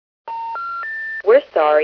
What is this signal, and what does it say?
Special information tone, with speech.